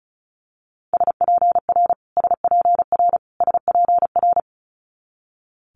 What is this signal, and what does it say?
Morse code, heard as a keyed tone, about 36 wpm, 700 Hz. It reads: SPR SPR SPR
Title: Multi-letter Phonemes Repeated 3x 35wpm